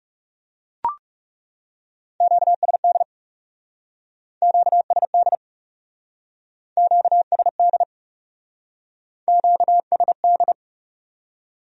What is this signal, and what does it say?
Morse code, heard as a keyed tone, about 38 wpm, 700 Hz. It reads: QSD QSD QSD QSD